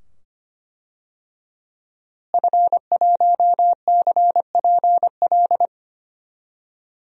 Morse code, heard as a keyed tone, about 25 wpm, 700 Hz. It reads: F1CPL
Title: DXCC-100 Call Signs - 5 Characters - Repeated 2x - 25wpm